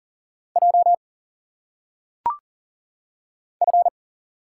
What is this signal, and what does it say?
Morse code, heard as a keyed tone, about 40 wpm, 700 Hz. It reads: J F
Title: Single Letters - Phonetic Pronunciation 40wpm